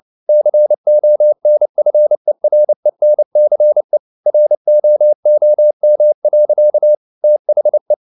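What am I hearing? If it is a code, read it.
Morse code, heard as a keyed tone, about 29 wpm, 600 Hz. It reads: CONFERENCE ROOM. THE
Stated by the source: Morse code at 29 wpm, sending Ken Follett's novel Never in English, from CW med Gnister.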